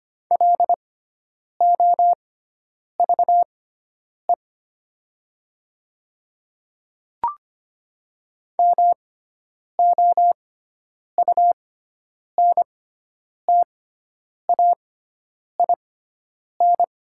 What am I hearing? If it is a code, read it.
Morse code, heard as a keyed tone, about 25 wpm, spaced as for 9 wpm, 700 Hz. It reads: LOVE MOUNTAIN
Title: 6x Character Spacing - Top 500 Words - 25wpm